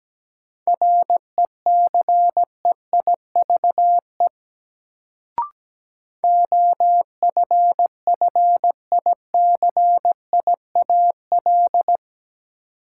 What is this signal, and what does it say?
Morse code, heard as a keyed tone, about 17 wpm, 700 Hz. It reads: RECEIVE OFFICIAL
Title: Sentences from Top 1000 Words - New Word Drill - 17wpm